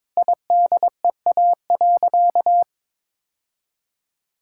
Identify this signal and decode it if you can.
Morse code, heard as a keyed tone, about 22 wpm, 700 Hz. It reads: IDEA.
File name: practice / 22wpm / Sentences with 4-Word Phrases - 22wpm